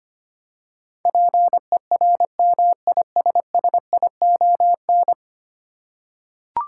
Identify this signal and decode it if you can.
Morse code, heard as a keyed tone, about 25 wpm, 700 Hz. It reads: PERMISSION
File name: Ten Letter Words - 25wpm